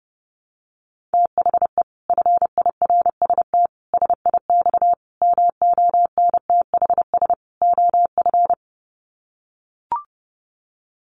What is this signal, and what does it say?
Morse code, heard as a keyed tone, about 30 wpm, 700 Hz. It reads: THE FIRST SIX MONTHS OF
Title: Sets of 5 Words Encore2 30wpm